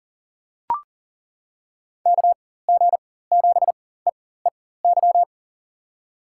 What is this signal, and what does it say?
Morse code, heard as a keyed tone, about 40 wpm, spaced as for 19 wpm, 700 Hz. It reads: KG7EEY
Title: USA Call Signs - 6 Characters - 4x Character Spacing - 40wpm